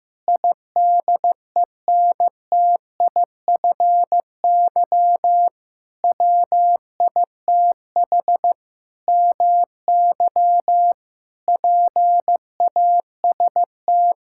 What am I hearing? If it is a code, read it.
Morse code, heard as a keyed tone, about 15 wpm, 700 Hz. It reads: IDENTIFY WITH MY PAST
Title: Sentences from Top 1000 Words - Review Part 6 - 15wpm